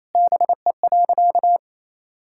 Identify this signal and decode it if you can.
Morse code, heard as a keyed tone, about 28 wpm, 700 Hz. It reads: BE.
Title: Sentences from Top 1000 Words - Review Part 6 - 28wpm